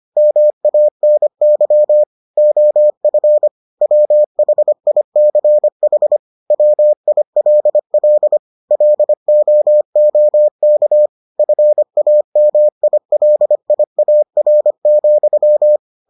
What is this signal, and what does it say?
Morse code, heard as a keyed tone, about 25 wpm, 600 Hz. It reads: MANY OF WHICH WILL LOOK FAMILIAR,